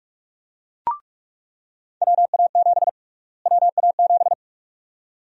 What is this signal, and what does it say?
Morse code, heard as a keyed tone, about 46 wpm, 700 Hz. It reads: WA7 WA7
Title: USA Call Sign Prefixes - Repeated 2x - 45wpm